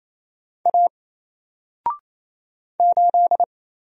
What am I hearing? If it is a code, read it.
Morse code, heard as a keyed tone, about 27 wpm, 700 Hz. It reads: A 8
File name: Single-Letter-Number-28wpm